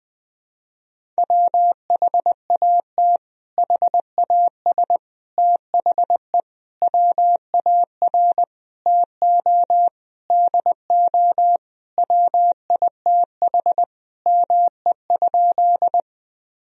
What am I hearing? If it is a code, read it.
Morse code, heard as a keyed tone, about 20 wpm, 700 Hz. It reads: WHAT HAS THE WAR TO DO WITH ME?
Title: Sentences from Top 1000 Words - Review Part 2 - 20wpm